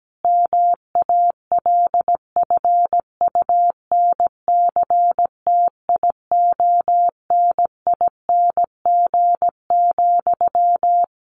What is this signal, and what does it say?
Morse code, heard as a keyed tone, about 17 wpm, 700 Hz. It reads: MALFUNCTIONING,
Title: A Sound Of Thunder 17wpm